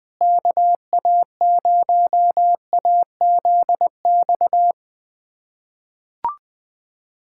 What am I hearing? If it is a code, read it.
Morse code, heard as a keyed tone, about 21 wpm, 700 Hz. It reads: KA0AZX
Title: USA Callsign DE Callsign 20wpm